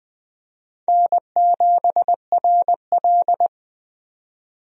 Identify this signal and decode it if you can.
Morse code, heard as a keyed tone, about 20 wpm, 700 Hz. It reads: N7RL
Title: USA Call Signs - 4 Characters - 20wpm